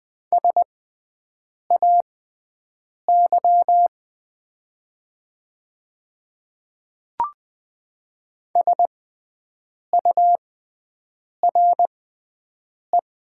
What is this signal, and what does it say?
Morse code, heard as a keyed tone, about 20 wpm, spaced as for 7 wpm, 700 Hz. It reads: SAY SURE
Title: 6x Character Spacing - Top 300 Words - 20wpm